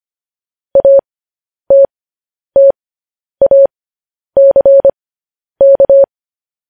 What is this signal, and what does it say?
Morse code, heard as a keyed tone, about 25 wpm, spaced as for 10 wpm, 550 Hz. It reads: ATTACK
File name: podcast / News headlines-1x-25x10